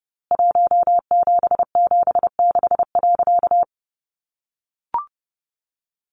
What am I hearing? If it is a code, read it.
Morse code, heard as a keyed tone, about 30 wpm, 700 Hz. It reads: 1776.
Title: A Sound Of Thunder 30wpm